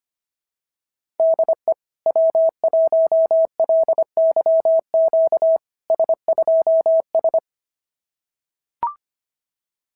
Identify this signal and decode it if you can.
Morse code, heard as a keyed tone, about 24 wpm, 650 Hz. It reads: DE W1LYQ S2S
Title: SOTA QSO Practice 25wpm